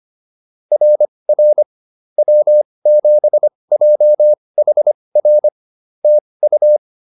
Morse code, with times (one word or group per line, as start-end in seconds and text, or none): 0.7-1.6 RR
2.2-5.5 W7JHR
6.0-6.8 TU